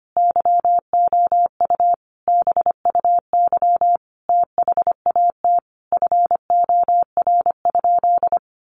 Morse code, heard as a keyed tone, about 25 wpm, 700 Hz. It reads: YOU BUY THAT FOR?
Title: Sentences from Top 700 Words 25wpm